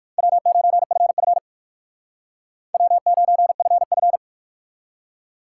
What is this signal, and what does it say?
Morse code, heard as a keyed tone, about 59 wpm, 700 Hz. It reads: W9FF W9FF